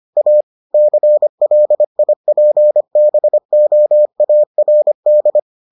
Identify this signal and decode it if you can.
Morse code, heard as a keyed tone, about 25 wpm, 600 Hz. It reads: A CLIPBOARD